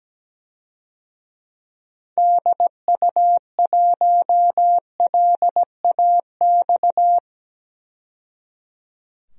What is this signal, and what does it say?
Morse code, heard as a keyed tone, about 17 wpm, 700 Hz. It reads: DU1LAX
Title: DXCC-100 Call Signs - 6 Characters - 17wpm